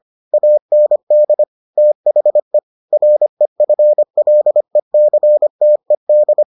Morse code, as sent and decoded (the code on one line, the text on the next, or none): .- -. -.. / - .... / .-. ..-. .-.. -.-. - . -..
AND THE REFLECTED